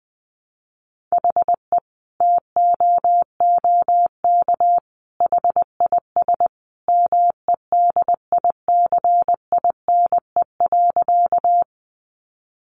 Morse code, heard as a keyed tone, about 20 wpm, 700 Hz. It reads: HE TOOK HIS MEDICINE.
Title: Sentences from Top 900 Words 20wpm